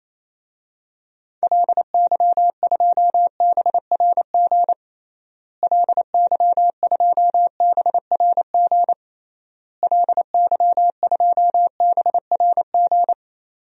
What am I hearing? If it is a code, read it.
Morse code, heard as a keyed tone, about 28 wpm, 700 Hz. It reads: LY2BRG LY2BRG LY2BRG
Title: DXCC-100 Call Signs - 6 Characters - Repeated 3x - 28wpm